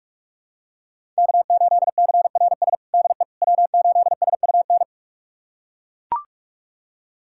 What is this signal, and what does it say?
Morse code, heard as a keyed tone, about 41 wpm, 700 Hz. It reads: K8CRS DE W8SUN
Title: USA Callsign DE Callsign - 5 Characters - 45wpm